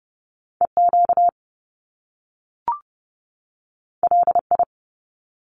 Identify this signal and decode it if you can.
Morse code, heard as a keyed tone, about 30 wpm, 700 Hz. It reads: EQ LI
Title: Top Two Letter Combinations in Words 30wpm